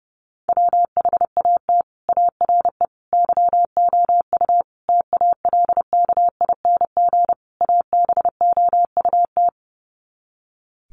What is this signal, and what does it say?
Morse code, heard as a keyed tone, about 30 wpm, 700 Hz. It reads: WHAT ARE YOU TALKING ABOUT
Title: Sets of 5 Words 30wpm